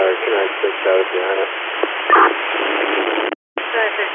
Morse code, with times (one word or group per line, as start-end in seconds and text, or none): none